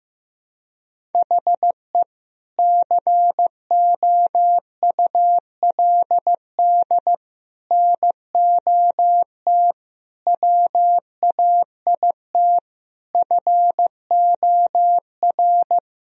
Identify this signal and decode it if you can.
Morse code, heard as a keyed tone, about 15 wpm, 700 Hz. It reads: HE COULD NOT WAIT FOR
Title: Sentences from Top 1000 Words - Review Part 1 - 15wpm